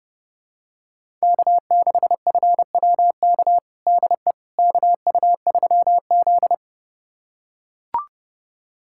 Morse code, heard as a keyed tone, about 29 wpm, 700 Hz. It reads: K6FWK DE KU3Z
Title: USA Callsign DE Callsign 30wpm